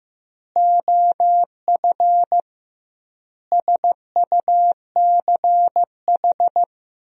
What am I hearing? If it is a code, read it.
Morse code, heard as a keyed tone, about 15 wpm, 700 Hz. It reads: OF SUCH
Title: 2x Word-Spacing - Sentences from Top 200 Words - 15wpm